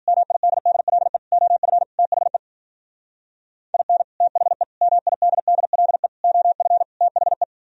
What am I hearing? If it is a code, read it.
Morse code, heard as a keyed tone, about 50 wpm, 700 Hz. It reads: MIDDLE OF THE IN THE MIDDLE OF THE
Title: Speed-Racing - Sets of 5 Words 40wpm